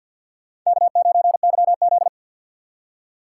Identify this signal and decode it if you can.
Morse code, heard as a keyed tone, about 52 wpm, 700 Hz. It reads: K9YZ
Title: USA Call Signs - 4 Characters - 6x Character Spacing - 50wpm